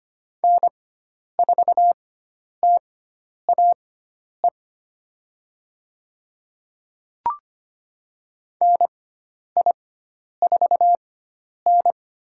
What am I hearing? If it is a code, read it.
Morse code, heard as a keyed tone, about 25 wpm, spaced as for 10 wpm, 700 Hz. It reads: N4TAE NI4N
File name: Lesson 16g - Letter c - Call Signs 25wpm